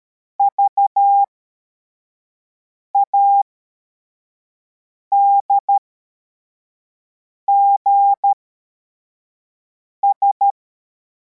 Morse code, heard as a keyed tone, about 13 wpm, spaced as for 4 wpm, 800 Hz. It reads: VADGS
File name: formation / cw / Text52